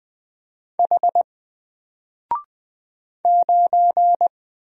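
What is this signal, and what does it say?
Morse code, heard as a keyed tone, about 20 wpm, 700 Hz. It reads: H 9
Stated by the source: Single letter and number Morse drill at 20 wpm, from N4LCD.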